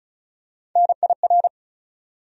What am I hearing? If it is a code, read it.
Morse code, heard as a keyed tone, about 36 wpm, 700 Hz. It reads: NIR
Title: USA Call Sign Postfixes - 3 Letters 35wpm